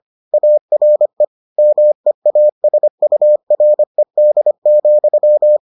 Morse code, heard as a keyed tone, about 25 wpm, 600 Hz. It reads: ARE MEASURED,